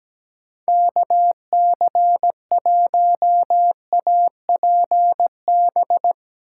Morse code, heard as a keyed tone, about 17 wpm, 700 Hz. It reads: KC1APB
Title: USA Call Signs - 6 Characters - 8x Character Spacing - 17wpm